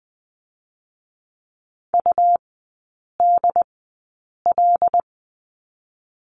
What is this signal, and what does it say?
Morse code, heard as a keyed tone, about 20 wpm, 700 Hz. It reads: U D L
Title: 114 Lesson 19e - Letter w - Three Characters 20wpm